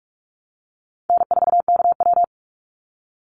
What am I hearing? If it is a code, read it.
Morse code, heard as a keyed tone, about 44 wpm, 700 Hz. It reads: N4KW
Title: Easy Call Signs Repeated 3x - 45wpm